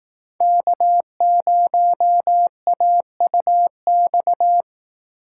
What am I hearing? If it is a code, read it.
Morse code, heard as a keyed tone, about 19 wpm, 700 Hz. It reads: K0AUX
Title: Speed Racing - USA 1x3 Call Signs 15wpm